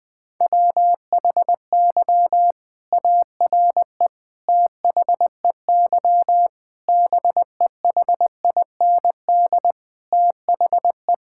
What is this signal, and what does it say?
Morse code, heard as a keyed tone, about 20 wpm, 700 Hz. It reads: WHY ARE THEY BEHIND THE
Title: Sentences from Top 1000 Words - Review Part 6 - 20wpm